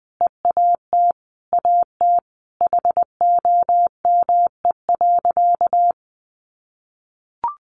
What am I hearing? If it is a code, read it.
Morse code, heard as a keyed tone, about 19 wpm, 700 Hz. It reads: EAT AT HOME.
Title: Sentences from Top 1400 Words 20wpm